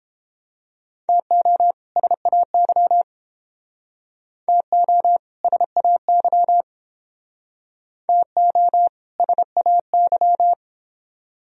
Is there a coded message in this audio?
Yes: Morse code from 1.1 to 10.5 s, heard as a keyed tone, about 30 wpm, 700 Hz. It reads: TO SAY TO SAY TO SAY